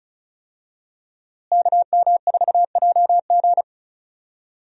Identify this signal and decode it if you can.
Morse code, heard as a keyed tone, about 35 wpm, 700 Hz. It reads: KM4JG
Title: USA Call Signs - 5 Characters - Repeated 2x - 35wpm